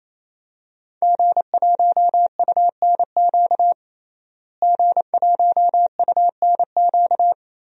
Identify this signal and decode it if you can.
Morse code, heard as a keyed tone, about 29 wpm, 700 Hz. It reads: G1UNQ G1UNQ
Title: DXCC-100 Call Signs - 5 Characters - Repeated 3x - 28wpm